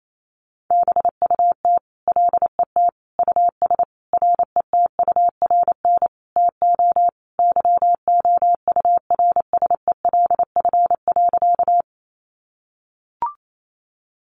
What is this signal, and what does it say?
Morse code, heard as a keyed tone, about 28 wpm, 700 Hz. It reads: BUT LET US RETURN TO YOURSELF.
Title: Sentences from Top 800 Words 28wpm